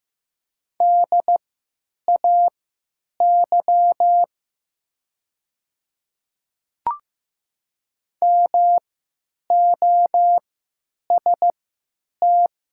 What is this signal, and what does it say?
Morse code, heard as a keyed tone, about 15 wpm, spaced as for 9 wpm, 700 Hz. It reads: DAY MOST
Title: 3x Character Spacing - Top 100 Words - 15wpm